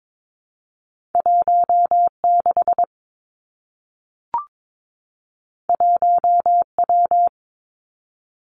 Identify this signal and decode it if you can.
Morse code, heard as a keyed tone, about 22 wpm, 700 Hz. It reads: 1 6 1 W
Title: Two Challenging Letter-Number Combinations 22wpm